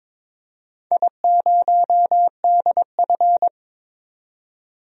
Morse code, heard as a keyed tone, about 22 wpm, 700 Hz. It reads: I0DF
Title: DXCC-100 Call Signs - 4 Characters - 22wpm